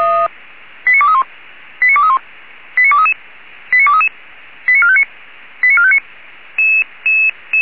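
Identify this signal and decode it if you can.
Five-tone selective call (ZVEI), heard as quick runs of five tones, tones of about 70 ms. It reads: 89231 89231 89233 89233 89459 89459
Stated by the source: ZVEI five-tone call 89241 received on SDR.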